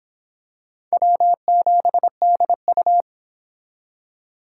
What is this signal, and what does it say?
Morse code, heard as a keyed tone, about 26 wpm, 700 Hz. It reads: W7DU